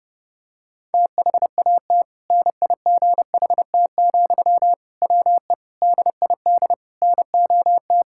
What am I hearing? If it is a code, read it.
Morse code, heard as a keyed tone, about 30 wpm, 700 Hz. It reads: THAT NIGHT, WE DID NOT
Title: Sentences from Top 1000 Words - Review Part 6 - 30wpm